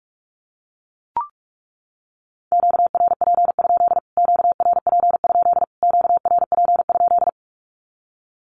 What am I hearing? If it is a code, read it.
Morse code, heard as a keyed tone, about 44 wpm, 700 Hz. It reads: QRP? QRP? QRP?
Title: Q-Codes Repeated 3x 45wpm